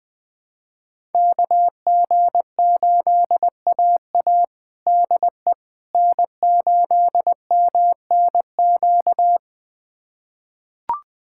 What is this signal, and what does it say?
Morse code, heard as a keyed tone, about 20 wpm, 700 Hz. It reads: KG8AA DE N8MNQ
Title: USA Callsign DE Callsign - 5 Characters - 20wpm